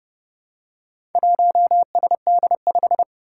Morse code, heard as a keyed tone, about 28 wpm, 700 Hz. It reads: 1SD5
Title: Four Challenging Letter-Number Combinations 30wpm